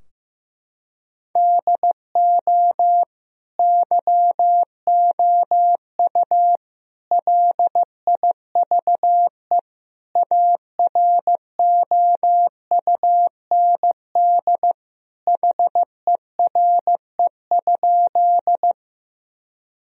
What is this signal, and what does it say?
Morse code, heard as a keyed tone, about 15 wpm, 700 Hz. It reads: DO YOU LIVE AROUND HERE?
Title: Sentences from Top 1000 Words - Review Part 3 - 15wpm